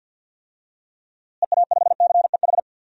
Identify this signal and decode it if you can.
Morse code, heard as a keyed tone, about 53 wpm, 700 Hz. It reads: EA5CEH
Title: DXCC-100 Call Signs - 6 Characters - Repeated 2x - 50wpm